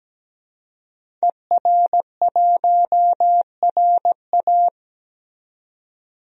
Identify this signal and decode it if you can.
Morse code, heard as a keyed tone, about 17 wpm, 700 Hz. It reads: ER1RA
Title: DXCC-100 Call Signs - 5 Characters - 17wpm